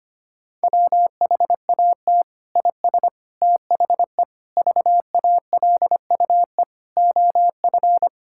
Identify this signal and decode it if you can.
Morse code, heard as a keyed tone, about 25 wpm, 700 Hz. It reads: WHAT IS THE VALUE OF
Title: Sentences from Top 1000 Words - Review Part 6 - 25wpm